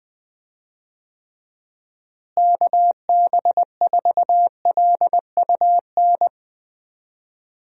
Morse code, heard as a keyed tone, about 21 wpm, 700 Hz. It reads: KB4LUN
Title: USA Call Signs - 6 Characters - Repeated 2x - 20wpm